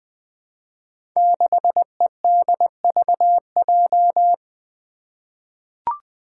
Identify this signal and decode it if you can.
Morse code, heard as a keyed tone, about 20 wpm, 700 Hz. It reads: 6EDVJ
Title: Code Groups - 5 Letter:Numbers - 20wpm